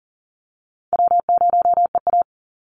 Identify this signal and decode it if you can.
Morse code, heard as a keyed tone, about 40 wpm, 700 Hz. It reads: W0EA
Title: Call Signs Repeated 3x - 40wpm